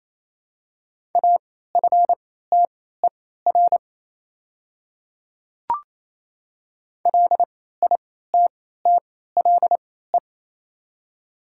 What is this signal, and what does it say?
Morse code, heard as a keyed tone, about 28 wpm, spaced as for 16 wpm, 700 Hz. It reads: AFTER LITTLE
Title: 3x Character Spacing - Top 300 Words - 28wpm